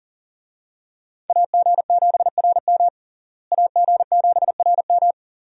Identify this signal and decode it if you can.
Morse code, heard as a keyed tone, about 37 wpm, 700 Hz. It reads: AG7RM AG7RM
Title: USA Call Signs - 5 Characters - Repeated 3x - 40wpm